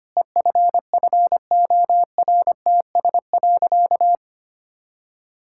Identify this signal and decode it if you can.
Morse code, heard as a keyed tone, about 25 wpm, 700 Hz. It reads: EFFORTS.
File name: practice / 25wpm / Difficult Sentences with Hints - 25wpm